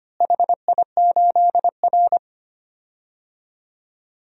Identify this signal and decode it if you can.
Morse code, heard as a keyed tone, about 25 wpm, 700 Hz. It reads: HI8R
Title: DXCC-100 Call Signs - 4 Characters - Repeated 2x - 25wpm